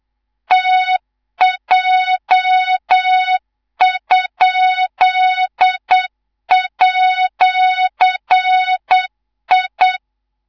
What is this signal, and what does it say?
Morse code, heard as a keyed tone, about 8 wpm, 3700 Hz, with noise. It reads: TJ?@I
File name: Morze-HA2MN / hang / 19 GYAKORLAT